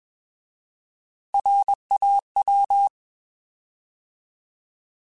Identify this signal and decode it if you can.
Morse code, heard as a keyed tone, about 21 wpm, 800 Hz. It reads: RAW
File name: Alphabetical / Alphabetical mp3s / Rs @20